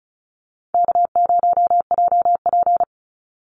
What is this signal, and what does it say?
Morse code, heard as a keyed tone, about 34 wpm, 700 Hz. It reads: K0JP
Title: Easy Call Signs Repeated 3x - 35wpm